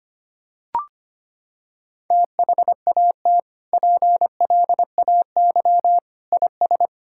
Morse code, heard as a keyed tone, about 25 wpm, 700 Hz. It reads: THAT PLAY IS
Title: Sentences from Top 1000 Words - Review Part 5 - 25wpm